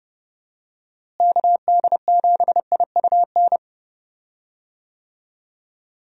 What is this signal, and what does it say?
Morse code, heard as a keyed tone, about 30 wpm, 700 Hz. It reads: KD7IUN
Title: USA Call Signs - 6 Characters - Repeated 3x - 30wpm